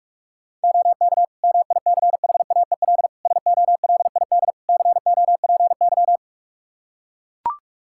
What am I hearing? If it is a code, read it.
Morse code, heard as a keyed tone, about 43 wpm, 700 Hz. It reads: OK MICHAEL SOLID COPY
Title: K1USN SST QSO Practice 45wpm